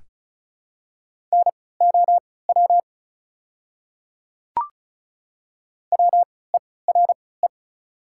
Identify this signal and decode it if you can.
Morse code, heard as a keyed tone, about 36 wpm, spaced as for 20 wpm, 690 Hz. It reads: NOW WERE
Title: 3x Character Spacing - Top 100 Words - 35wpm